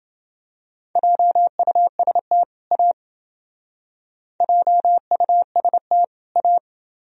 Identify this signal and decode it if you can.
Morse code, heard as a keyed tone, about 30 wpm, 700 Hz. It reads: JUST A JUST A